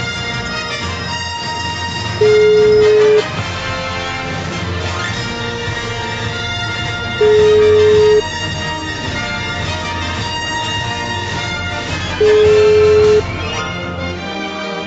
Ringback tone (Europe, 425 Hz), with noise.